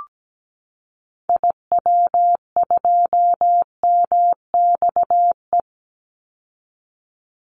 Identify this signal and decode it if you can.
Morse code, heard as a keyed tone, about 17 wpm, 700 Hz. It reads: IW2MXE
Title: Call Signs Repeated 2x - 17wpm